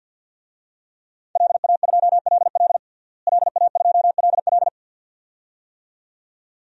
Morse code, heard as a keyed tone, about 51 wpm, 700 Hz. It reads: LA2LL LA2LL
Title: DXCC-100 Call Signs - 5 Characters - Repeated 2x - 50wpm